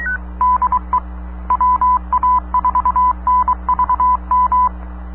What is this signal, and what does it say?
Morse code, heard as a keyed tone, about 23 wpm, 1000 Hz. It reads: DE WA4NVM
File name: sat / AO-51 14Jul2010 SSTV